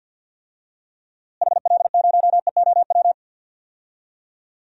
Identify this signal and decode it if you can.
Morse code, heard as a keyed tone, about 50 wpm, 700 Hz. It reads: HL0EOW
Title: DXCC-100 Call Signs - 6 Characters - Repeated 2x - 50wpm